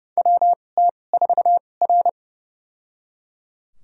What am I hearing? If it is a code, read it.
Morse code, heard as a keyed tone, about 30 wpm, spaced as for 22 wpm, 700 Hz. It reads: WT4R
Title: USA Call Signs - 4 Characters - 2x Character Spacing - 30wpm